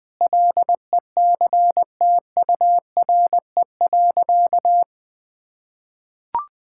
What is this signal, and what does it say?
Morse code, heard as a keyed tone, about 20 wpm, 700 Hz. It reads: LECTURE.
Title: Sentences from Top 1600 Words 20wpm